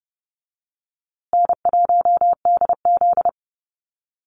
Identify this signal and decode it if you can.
Morse code, heard as a keyed tone, about 30 wpm, 700 Hz. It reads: N1DZ
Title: USA Call Signs Repeated 2x 30wpm